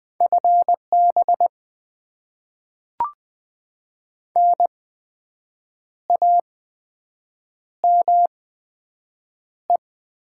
Morse code, heard as a keyed tone, about 20 wpm, spaced as for 7 wpm, 700 Hz. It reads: FB NAME
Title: Top 20 QSO Elements - 8x Character Spacing - 20wpm